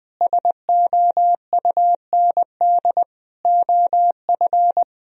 Morse code, heard as a keyed tone, about 21 wpm, 700 Hz. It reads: SOUND OF
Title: Sentences from Top 1000 Words - Review Part 1 - 20wpm